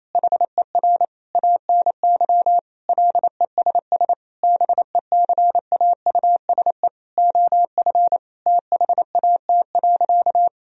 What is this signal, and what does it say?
Morse code, heard as a keyed tone, about 28 wpm, 700 Hz. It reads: HER ANY LESS BECAUSE OF THAT.
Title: Sentences from Top 1000 Words - Review Part 2 - 28wpm